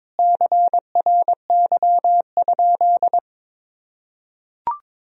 Morse code, heard as a keyed tone, about 22 wpm, 700 Hz. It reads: CRY?
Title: Sentences from Top 1000 Words - Review Part 2 - 22wpm